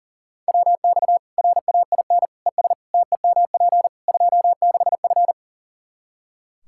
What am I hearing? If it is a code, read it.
Morse code, heard as a keyed tone, about 40 wpm, 700 Hz. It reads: WX RAIN ES TEMP 26F
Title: Practice QSO - Exchange 2 - Info - 40wpm